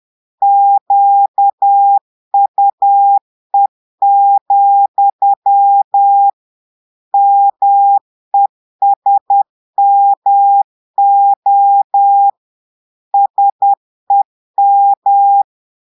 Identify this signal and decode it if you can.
Morse code, heard as a keyed tone, about 10 wpm, 800 Hz. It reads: QUE, MESMO SEM
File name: Portugais - texte en clair 1 - vitesse - 10 mots minute